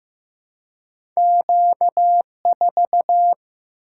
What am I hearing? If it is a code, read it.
Morse code, heard as a keyed tone, about 15 wpm, 700 Hz. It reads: Q4
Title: Single-Letter Single-Number 15wpm